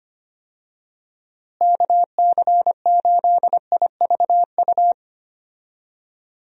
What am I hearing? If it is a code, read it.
Morse code, heard as a keyed tone, about 25 wpm, 700 Hz. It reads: KC8IVU
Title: USA Call Signs - 6 Characters - Repeated 3x - 25wpm